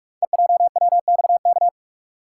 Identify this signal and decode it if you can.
Morse code, heard as a keyed tone, about 44 wpm, 700 Hz. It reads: EJWXK
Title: Code Groups - 5 Letters - 45wpm